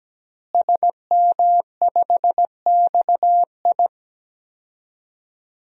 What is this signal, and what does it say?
Morse code, heard as a keyed tone, about 17 wpm, 700 Hz. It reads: SM5XI
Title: DXCC-100 Call Signs - 5 Characters - Repeated 2x - 17wpm